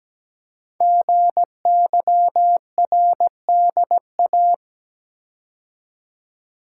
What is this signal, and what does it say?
Morse code, heard as a keyed tone, about 17 wpm, 700 Hz. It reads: GYRDA
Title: Code Groups - 5 Letters - 17wpm